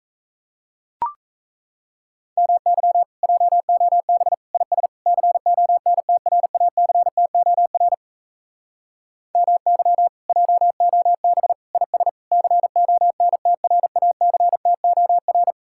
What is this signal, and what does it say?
Morse code, heard as a keyed tone, about 37 wpm, 700 Hz. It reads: MY JOB IS CONTRACTOR MY JOB IS CONTRACTOR